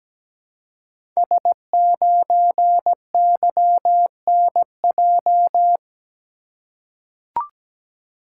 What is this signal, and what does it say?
Morse code, heard as a keyed tone, about 17 wpm, 700 Hz. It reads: S9YNJ
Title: Code Groups - 5 Letter:Numbers - 17wpm